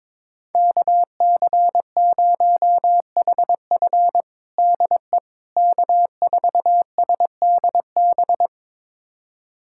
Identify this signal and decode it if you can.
Morse code, heard as a keyed tone, about 22 wpm, 700 Hz. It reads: KC0HF DE K4SDB
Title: USA Callsign DE Callsign - 5 Characters - 22wpm